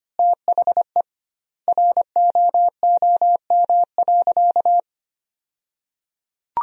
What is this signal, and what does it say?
Morse code, heard as a keyed tone, about 25 wpm, 700 Hz. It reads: THE ROOM.
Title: 2x Word-Spacing - Sentences from Top 400 Words - 25wpm